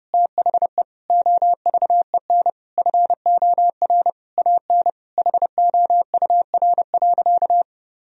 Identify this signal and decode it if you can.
Morse code, heard as a keyed tone, about 30 wpm, 700 Hz. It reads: THE OVEN FOR AN HOUR.
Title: Sentences from Top 1000 Words - Review Part 4 - 30wpm